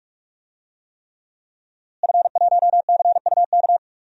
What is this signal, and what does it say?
Morse code, heard as a keyed tone, about 46 wpm, 700 Hz. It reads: F1CUK
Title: DXCC-100 Call Signs - 5 Characters - Repeated 3x - 45wpm